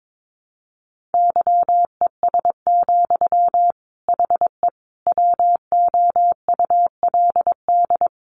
Morse code, heard as a keyed tone, about 22 wpm, 700 Hz. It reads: YES, HE WOULD
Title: Sentences from Top 900 Words 22wpm